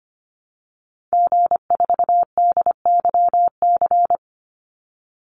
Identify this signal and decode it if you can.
Morse code, heard as a keyed tone, about 25 wpm, 700 Hz. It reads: G4DYC
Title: Call Signs Repeated 2x - 25wpm